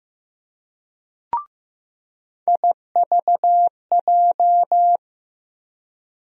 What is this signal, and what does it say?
Morse code, heard as a keyed tone, about 15 wpm, 700 Hz. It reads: IVJ
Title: USA Call Sign Postfixes - 3 Letters 15wpm